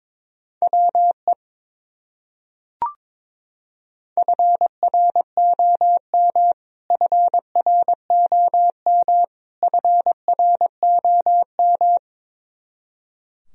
Morse code, heard as a keyed tone, about 22 wpm, 700 Hz. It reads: WE FROM FROM FROM
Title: Top 100 Words Repeated 22wpm